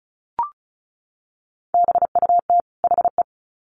Morse code, heard as a keyed tone, about 34 wpm, 700 Hz. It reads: BUT HE
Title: Sets of 2 Words 35wpm